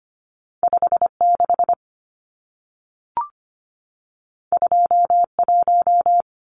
Morse code, heard as a keyed tone, about 25 wpm, 700 Hz. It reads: 56 21
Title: Numbers - 2-Digit 25wpm